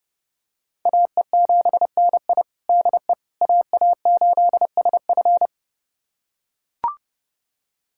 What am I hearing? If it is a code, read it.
Morse code, heard as a keyed tone, about 32 wpm, 700 Hz. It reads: AE7NI DE AA8SF
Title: USA Callsign DE Callsign - 5 Characters - 30wpm